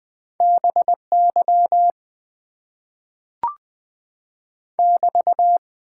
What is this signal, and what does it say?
Morse code, heard as a keyed tone, about 20 wpm, 700 Hz. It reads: BY <BT>
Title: Top 90 QSO Elements - 8x Character Spacing - 20wpm